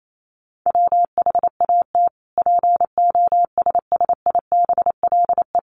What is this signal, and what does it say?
Morse code, heard as a keyed tone, about 28 wpm, 700 Hz. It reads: WHAT POSSIBLE
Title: Sentences from Top 1300 Words 28wpm